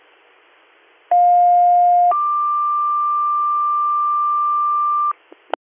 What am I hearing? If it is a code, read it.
Two-tone page, heard as a one-second tone, then a three-second tone, about 710 then 1180 Hz, with noise. It is heard from 1.1 to 5.1 s.